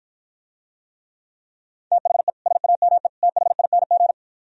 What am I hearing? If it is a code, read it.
Morse code, heard as a keyed tone, about 54 wpm, 700 Hz. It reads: THE SAME THING